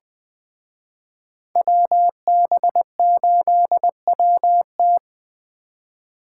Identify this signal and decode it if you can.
Morse code, heard as a keyed tone, about 20 wpm, 700 Hz. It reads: WB8WT
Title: Speed Racing - USA 2x2 Call Signs 17wpm